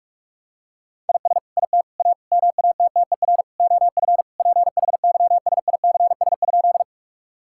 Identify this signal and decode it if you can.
Morse code, heard as a keyed tone, about 46 wpm, 700 Hz. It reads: IS IT A MATTER OF PHYSICS?
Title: Sentences from Top 1000 Words - Review Part 5 - 45wpm